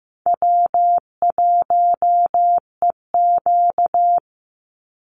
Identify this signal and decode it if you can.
Morse code, heard as a keyed tone, about 15 wpm, 700 Hz. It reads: W1EQ
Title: Easy Call Signs Repeated 2x - 15wpm